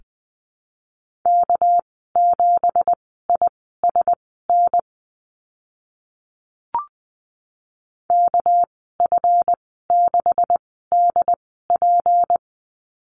Morse code, heard as a keyed tone, about 20 wpm, spaced as for 14 wpm, 700 Hz. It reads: K7ISN KF6DP
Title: 207 Lesson 33g - Letter j - Call Signs 20wpm